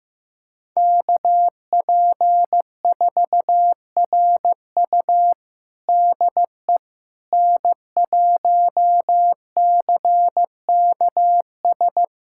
Morse code, heard as a keyed tone, about 15 wpm, 700 Hz. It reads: KP4RU DE N1CKS
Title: USA Callsign DE Callsign - 5 Characters - 15wpm